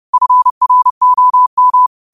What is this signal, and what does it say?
Morse code, heard as a keyed tone, about 30 wpm, 1000 Hz. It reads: FROM